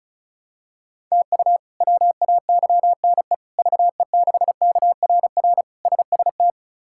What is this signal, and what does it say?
Morse code, heard as a keyed tone, about 34 wpm, 700 Hz. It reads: TU WAYNE VE6KRR SST
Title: K1USN SST QSO Practice - 35wpm